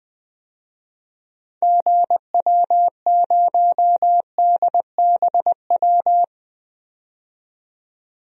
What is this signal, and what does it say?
Morse code, heard as a keyed tone, about 21 wpm, 700 Hz. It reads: GW0DBW
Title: DXCC-100 Call Signs - 6 Characters - 20wpm